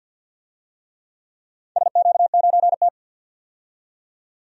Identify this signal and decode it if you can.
Morse code, heard as a keyed tone, about 45 wpm, 700 Hz. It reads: SQ9T
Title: DXCC-100 Call Signs - 4 Characters - 50wpm